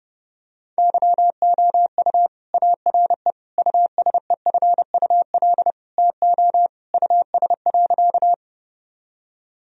Morse code, heard as a keyed tone, about 30 wpm, 700 Hz. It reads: YOU ARE USEFUL TO US.
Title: Sentences from Top 1000 Words - Review Part 4 - 30wpm